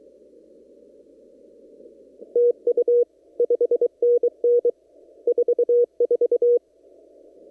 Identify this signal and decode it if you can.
Morse code, heard as a keyed tone, about 23 wpm, 480 Hz. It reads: TU 5NN 44
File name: CW / kt5j wpx cw-op-k5pi